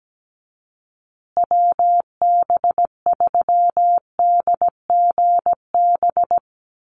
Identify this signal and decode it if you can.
Morse code, heard as a keyed tone, about 17 wpm, 700 Hz. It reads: WB3DGB
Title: USA Call Signs Repeated 3x 17wpm